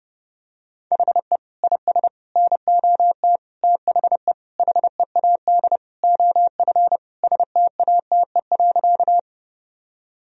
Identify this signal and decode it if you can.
Morse code, heard as a keyed tone, about 30 wpm, 700 Hz. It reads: HE IS NOT THE HEAD OF STATE.
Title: Sentences from Top 1000 Words - Review Part 2 - 30wpm